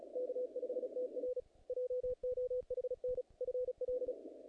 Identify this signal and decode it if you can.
Morse code, heard as a keyed tone, about 35 wpm, 500 Hz. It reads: K5TR JOHN FL
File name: contests / naqp-cw-k5tr-op-k5pi- -2000